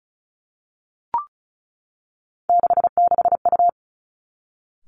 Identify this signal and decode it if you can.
Morse code, heard as a keyed tone, about 36 wpm, 700 Hz. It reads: 66U
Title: Three Challenging Letter-Number Combinations 35wpm